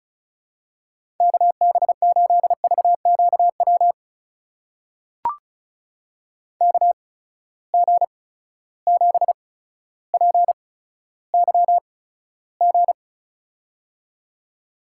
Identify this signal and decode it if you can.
Morse code, heard as a keyed tone, about 36 wpm, spaced as for 15 wpm, 700 Hz. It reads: KB8VQW KG7PYG
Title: USA Call Signs - 6 Characters - 8x Character Spacing - 35wpm